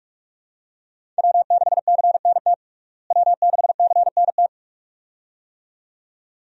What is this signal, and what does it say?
Morse code, heard as a keyed tone, about 47 wpm, 700 Hz. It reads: W6CNT W6CNT
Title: USA Call Signs - 5 Characters - Repeated 2x - 45wpm